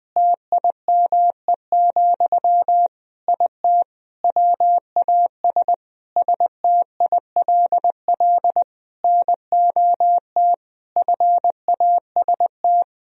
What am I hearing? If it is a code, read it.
Morse code, heard as a keyed tone, about 21 wpm, 700 Hz. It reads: TIME, IT WAS STILL NOT FAST
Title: Sentences from Top 1000 Words - Review Part 3 - 20wpm